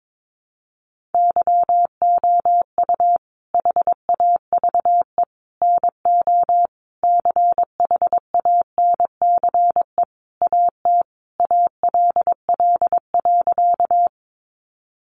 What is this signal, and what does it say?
Morse code, heard as a keyed tone, about 22 wpm, 700 Hz. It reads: YOU HAVE NO CHANCE AT ALL.
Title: Sentences from Top 800 Words 22wpm